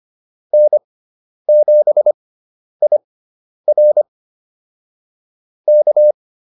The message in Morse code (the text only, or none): N7IR K